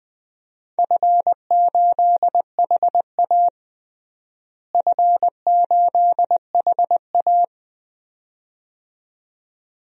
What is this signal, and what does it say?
Morse code, heard as a keyed tone, about 20 wpm, 700 Hz. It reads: F8HA F8HA
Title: DXCC-100 Call Signs - 4 Characters - Repeated 2x - 20wpm